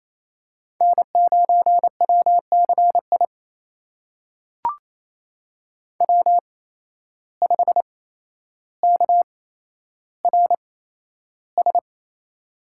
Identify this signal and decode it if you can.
Morse code, heard as a keyed tone, about 27 wpm, spaced as for 12 wpm, 700 Hz. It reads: N9WCI W5KRS
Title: USA Call Signs - 5 Characters - 8x Character Spacing - 28wpm